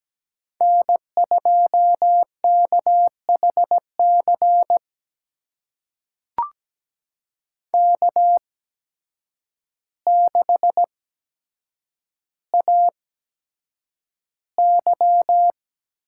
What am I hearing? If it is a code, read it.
Morse code, heard as a keyed tone, about 17 wpm, spaced as for 8 wpm, 700 Hz. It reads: N2KHC K6AY
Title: USA Call Signs - 8x Character Spacing - 17wpm